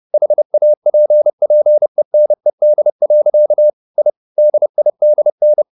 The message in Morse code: HAPPENED. I DIDN

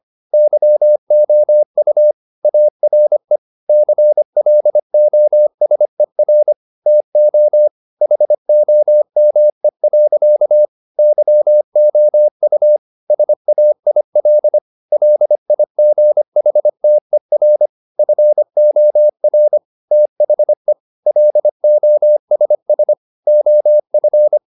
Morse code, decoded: YOU ARE CLOSER TO HOME. YOU SAIL LIGHTER FOR THE LOSS OF

